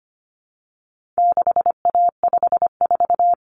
Morse code, heard as a keyed tone, about 25 wpm, 700 Hz. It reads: 6A54
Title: Common Cut Numbers - 4-Digit 25wpm